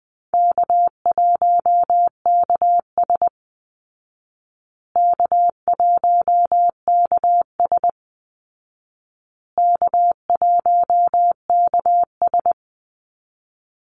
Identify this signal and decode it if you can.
Morse code, heard as a keyed tone, about 20 wpm, 700 Hz. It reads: K1KS K1KS K1KS
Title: Easy Call Signs Repeated 3x - 20wpm